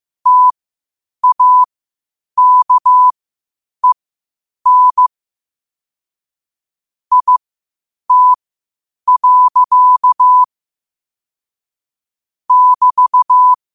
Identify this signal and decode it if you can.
Morse code, heard as a keyed tone, about 15 wpm, spaced as for 8 wpm, 1000 Hz. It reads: TAKEN IT. <BT>
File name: qotdfiles / 8wpm